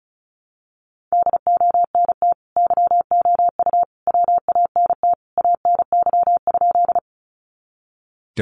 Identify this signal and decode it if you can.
Morse code, heard as a keyed tone, about 34 wpm, 700 Hz. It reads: DONT YOU WANT ANY?
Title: Sentences from Top 300 Words 35wpm